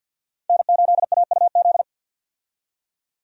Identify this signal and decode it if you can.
Morse code, heard as a keyed tone, about 50 wpm, 700 Hz. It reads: N8AUZ
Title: USA Call Signs - 5 Characters - 50wpm